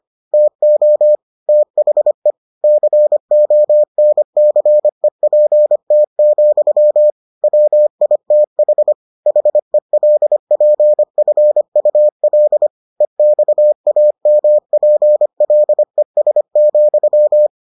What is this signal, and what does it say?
Morse code, heard as a keyed tone, about 25 wpm, 600 Hz. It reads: TO THE CONCEPT, WITH HELPFUL EXAMPLES,